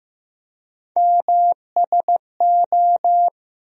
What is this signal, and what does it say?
Morse code, heard as a keyed tone, about 15 wpm, 700 Hz. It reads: MSO
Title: USA Call Sign Postfixes - 3 Letters 15wpm